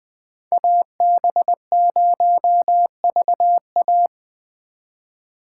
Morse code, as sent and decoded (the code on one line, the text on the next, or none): .- -... ----- ...- .-
AB0VA